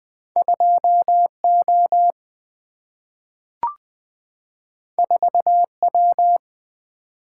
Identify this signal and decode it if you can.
Morse code, heard as a keyed tone, about 20 wpm, 700 Hz. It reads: 2 O 4 W
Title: Single-Letter Single-Number 20wpm